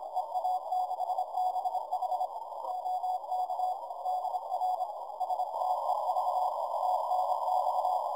Morse code, heard as a keyed tone, about 26 wpm, 740 Hz, with noise. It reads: EA6BH OK DR STEV TNX
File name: 30mtr dir noisereduction EA6BH CW